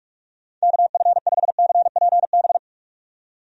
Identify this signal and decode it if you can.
Morse code, heard as a keyed tone, about 41 wpm, 700 Hz. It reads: KF5CPB
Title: USA Call Signs - 6 Characters - 45wpm